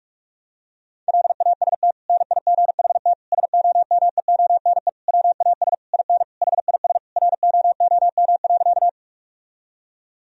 Morse code, heard as a keyed tone, about 46 wpm, 700 Hz. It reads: LAST NIGHT SOMEONE WAS IN HIS ROOM.
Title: Sentences from Top 1000 Words - Review Part 4 - 45wpm